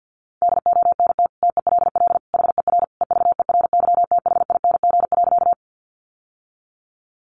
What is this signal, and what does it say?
Morse code, heard as a keyed tone, about 50 wpm, 700 Hz. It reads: DONT TELL HER EVERYTHING.
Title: Sentences from Top 700 Words 50wpm